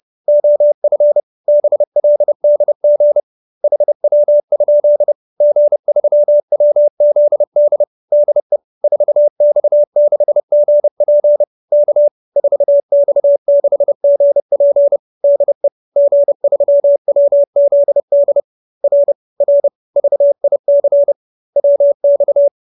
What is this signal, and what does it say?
Morse code, heard as a keyed tone, about 30 wpm, 570 Hz. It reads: OF BLDG HW? G3WZD DE 4X6GP K 4X6GP DE G3WZD R R VIC WX
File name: QSO310 30